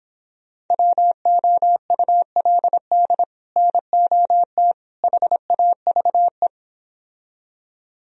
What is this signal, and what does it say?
Morse code, heard as a keyed tone, about 31 wpm, 700 Hz. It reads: WOULD NOT HAVE